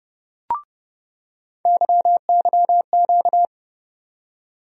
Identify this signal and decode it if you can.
Morse code, heard as a keyed tone, about 30 wpm, 700 Hz. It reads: YYQ
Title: USA Call Sign Postfixes - 3 Letters 30wpm